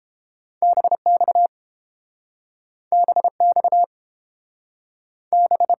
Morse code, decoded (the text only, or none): BX BX B